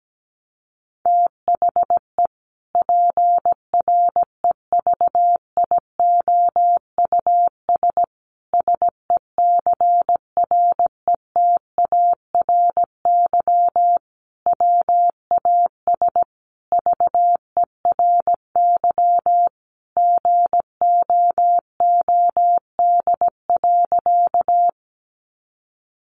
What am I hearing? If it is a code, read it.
Morse code, heard as a keyed tone, about 17 wpm, 700 Hz. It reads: THE PREVIOUS SECRETARY WAS VERY GOOD.
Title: Sentences from Top 1500 Words 17wpm